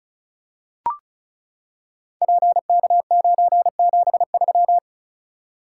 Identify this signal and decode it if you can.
Morse code, heard as a keyed tone, about 36 wpm, 690 Hz. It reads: PK973
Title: Code Groups - 5 Letter:Numbers - 35wpm